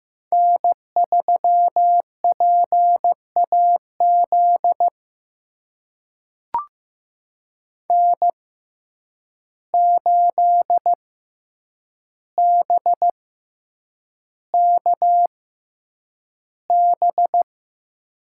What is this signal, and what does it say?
Morse code, heard as a keyed tone, about 15 wpm, spaced as for 8 wpm, 700 Hz. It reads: N3PAZ N8BKB
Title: USA Call Signs - 5 Characters - 6x Character Spacing - 15wpm